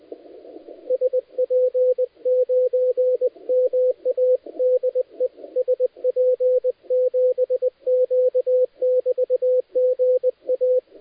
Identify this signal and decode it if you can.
Morse code, heard as a keyed tone, about 20 wpm, 510 Hz, with radio noise. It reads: SP9MA DE SP7Q <BT> GA
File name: dlazaawansowanych / nagraniazpasma / titawaNETcontrol